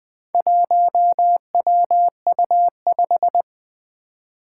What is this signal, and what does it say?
Morse code, heard as a keyed tone, about 20 wpm, 700 Hz. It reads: 1WU5
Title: Four Challenging Letter-Number Combinations 20wpm